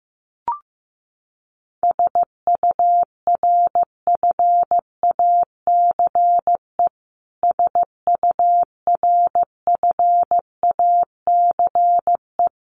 Morse code, heard as a keyed tone, about 15 wpm, 700 Hz. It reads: SURFACE SURFACE
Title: Top 500 Words Repeated 15wpm